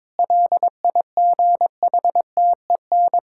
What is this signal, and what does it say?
Morse code, heard as a keyed tone, about 22 wpm, 700 Hz. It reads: LIGHTEN
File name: Difficult Sentences with Hints - 22wpm